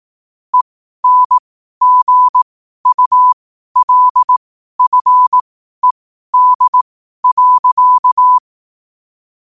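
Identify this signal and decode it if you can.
Morse code, heard as a keyed tone, about 18 wpm, spaced as for 12 wpm, 1000 Hz. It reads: ENGULFED.
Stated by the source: Morse code at 12 wpm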